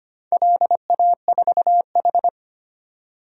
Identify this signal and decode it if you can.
Morse code, heard as a keyed tone, about 25 wpm, 700 Hz. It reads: LA4H
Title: DXCC-100 Call Signs - 4 Characters - 25wpm